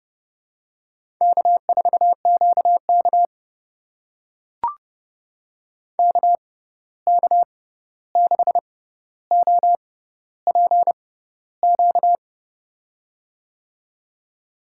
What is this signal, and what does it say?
Morse code, heard as a keyed tone, about 32 wpm, spaced as for 14 wpm, 700 Hz. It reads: K4QK KK6OPQ
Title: USA Call Signs - 6x Character Spacing - 30wpm